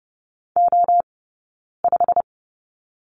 Morse code, heard as a keyed tone, about 30 wpm, 700 Hz. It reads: O 5
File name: course / 30:10wpm / 086 Lesson 15d - Number 5 - Two Characters 30wpm